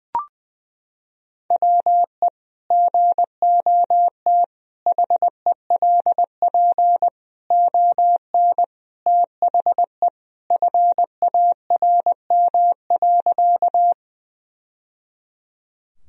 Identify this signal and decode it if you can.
Morse code, heard as a keyed tone, about 20 wpm, 700 Hz. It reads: WE GOT HELP ON THE FARM.
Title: Sentences from Top 1000 Words - Review Part 2 - 20wpm